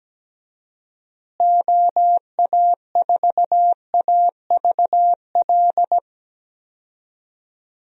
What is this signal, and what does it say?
Morse code, heard as a keyed tone, about 17 wpm, 700 Hz. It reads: OA4AVL
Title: USA Call Signs - 6 Characters - Repeated 3x - 17wpm